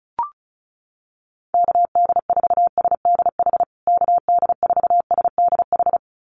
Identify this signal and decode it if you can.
Morse code, heard as a keyed tone, about 34 wpm, 700 Hz. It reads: KD4SDH KD4SDH
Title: USA Call Signs Repeated 2x 35wpm